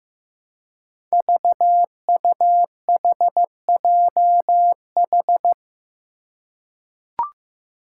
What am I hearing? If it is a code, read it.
Morse code, heard as a keyed tone, about 15 wpm, 700 Hz. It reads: VUHJH
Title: Five Challenging Letter-Number Combinations 15wpm